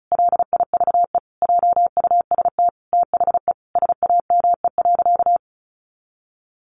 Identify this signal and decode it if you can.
Morse code, heard as a keyed tone, about 34 wpm, 700 Hz. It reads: LIVE JUST THE SAME.
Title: Sentences from Top 300 Words 35wpm